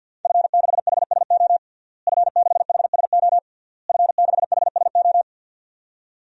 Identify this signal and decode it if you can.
Morse code, heard as a keyed tone, about 51 wpm, 700 Hz. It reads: F6HSO F6HSO F6HSO
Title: DXCC-100 Call Signs - 5 Characters - Repeated 3x - 50wpm